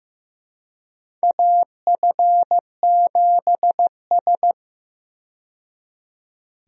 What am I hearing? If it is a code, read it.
Morse code, heard as a keyed tone, about 15 wpm, 700 Hz. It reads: AF7S
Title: USA Call Signs - 4 Characters - 15wpm